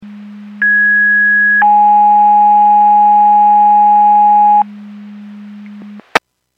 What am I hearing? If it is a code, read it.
Two-tone page, heard as a one-second tone, then a three-second tone, about 1650 then 830 Hz. It is heard from 0.6 to 4.6 s.
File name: Portage 154.385 Paging